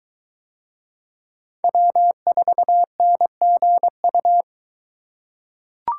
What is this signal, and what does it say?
Morse code, heard as a keyed tone, about 23 wpm, 700 Hz. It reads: W4NGU